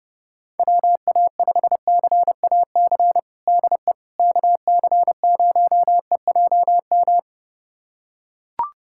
Morse code, heard as a keyed tone, about 30 wpm, 700 Hz. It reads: WA5CAC DE KC0EJM
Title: USA Callsign DE Callsign 30wpm